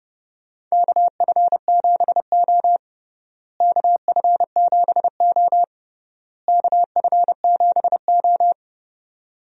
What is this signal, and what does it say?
Morse code, heard as a keyed tone, about 30 wpm, 700 Hz. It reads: KF7O KF7O KF7O
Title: USA Call Signs - 4 Characters - Repeated 3x - 30wpm